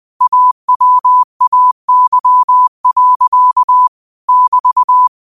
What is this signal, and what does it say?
Morse code, heard as a keyed tone, about 20 wpm, 1000 Hz. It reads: AWAY. <BT>
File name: qotdfiles / 20wpm